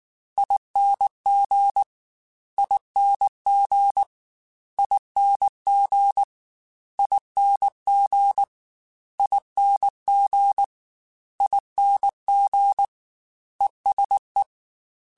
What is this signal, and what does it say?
Morse code, heard as a keyed tone, about 19 wpm, 800 Hz. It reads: ING ING ING ING ING ING ESE
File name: Special INGs-mp3s / Special INGs-mp3s / ESE ING @18